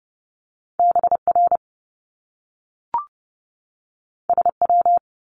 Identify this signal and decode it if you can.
Morse code, heard as a keyed tone, about 30 wpm, 700 Hz. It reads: BR SW
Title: Top Two Letter Combinations in Words 30wpm